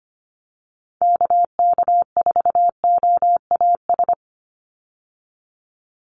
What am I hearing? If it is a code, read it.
Morse code, heard as a keyed tone, about 25 wpm, 690 Hz. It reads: KK4OAS
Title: USA Call Signs 25wpm